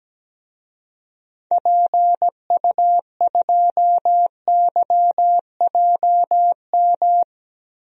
Morse code, heard as a keyed tone, about 17 wpm, 700 Hz. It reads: PU2YJM
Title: DXCC-100 Call Signs - 6 Characters - 17wpm